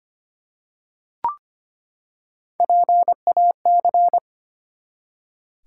Morse code, heard as a keyed tone, about 25 wpm, 700 Hz. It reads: PAC
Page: morse-code-ninja-practice-25wpm